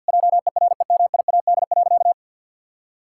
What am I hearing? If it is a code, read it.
Morse code, heard as a keyed tone, about 50 wpm, 700 Hz. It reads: JEREMIAD.
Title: Pangram Challenge v2 - 50wpm